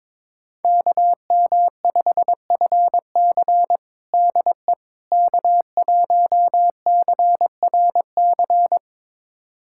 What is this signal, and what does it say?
Morse code, heard as a keyed tone, about 22 wpm, 700 Hz. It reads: KM5FC DE K1CRC
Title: USA Callsign DE Callsign - 5 Characters - 22wpm